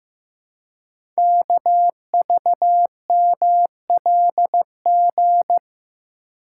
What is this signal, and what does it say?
Morse code, heard as a keyed tone, about 15 wpm, 700 Hz. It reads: KVMLG
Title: Code Groups - 5 Letters - 15wpm